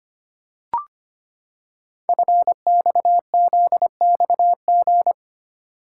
Morse code, heard as a keyed tone, about 25 wpm, 700 Hz. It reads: FXZXG
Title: Code Groups - 5 Letters - 25wpm